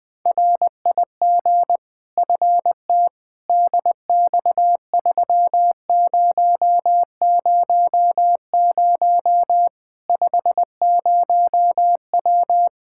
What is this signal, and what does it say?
Morse code, heard as a keyed tone, about 20 wpm, 700 Hz. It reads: RIG FT DX3000 50W
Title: Practice QSO - Exchange 2 - Info - 20wpm